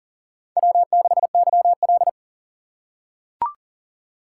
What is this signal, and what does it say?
Morse code, heard as a keyed tone, about 40 wpm, 700 Hz. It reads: W6YL
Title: USA Call Signs - 4 Characters - 6x Character Spacing - 40wpm